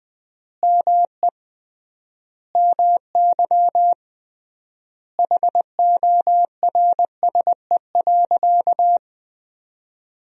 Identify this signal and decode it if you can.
Morse code, heard as a keyed tone, about 20 wpm, 700 Hz. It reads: ME MY HORSE.
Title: 3x Word-Spacing - Sentences from Top 500 Words - 20wpm